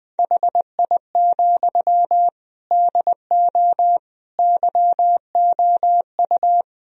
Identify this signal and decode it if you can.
Morse code, heard as a keyed tone, about 20 wpm, 700 Hz. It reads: HI, DO YOU
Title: Sentences from Top 1000 Words - Review Part 3 - 20wpm